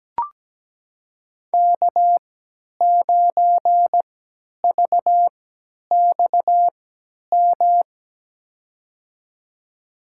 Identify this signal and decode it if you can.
Morse code, heard as a keyed tone, about 17 wpm, spaced as for 10 wpm, 700 Hz. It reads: K9VXM
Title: USA Call Signs - 5 Characters - 3x Character Spacing - 17wpm